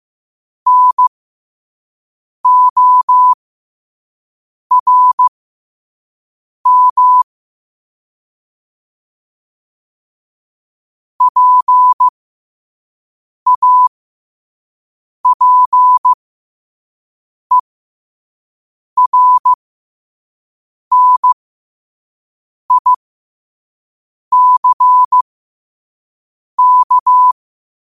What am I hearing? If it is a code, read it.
Morse code, heard as a keyed tone, about 15 wpm, spaced as for 5 wpm, 1000 Hz. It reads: NORM PAPERNICK
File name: qotdfiles / 5wpm